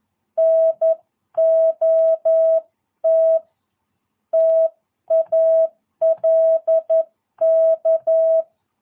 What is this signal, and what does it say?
Morse code, heard as a keyed tone, about 11 wpm, 650 Hz. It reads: NOT TALK
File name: jh8cbh / 2022cw